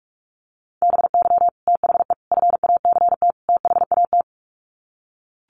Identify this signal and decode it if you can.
Morse code, heard as a keyed tone, about 44 wpm, 700 Hz. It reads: BY THE FACT THAT
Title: Sets of 4 Words 45wpm